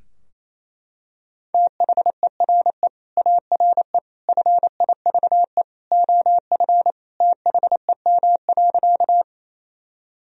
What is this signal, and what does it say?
Morse code, heard as a keyed tone, about 28 wpm, 700 Hz. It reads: THERE ARE FIVE OF THEM.
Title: Sentences from Top 1000 Words - Review Part 2 - 28wpm